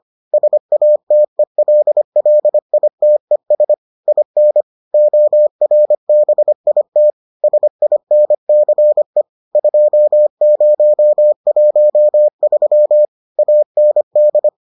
Morse code, heard as a keyed tone, about 25 wpm, 600 Hz. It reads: SATELLITES IN ORBIT SINCE 2013 AND